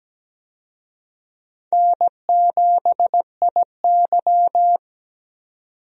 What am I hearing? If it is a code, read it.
Morse code, heard as a keyed tone, about 17 wpm, 700 Hz. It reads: N7IY
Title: USA Call Signs - 4 Characters - 17wpm